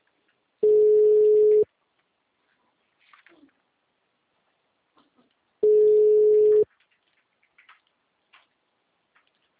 Ringback tone (Europe, 425 Hz).
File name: out- -2033- - - .95088